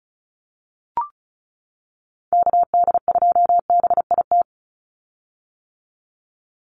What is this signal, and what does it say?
Morse code, heard as a keyed tone, about 34 wpm, 700 Hz. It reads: KD2BIT
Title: USA Call Signs 35wpm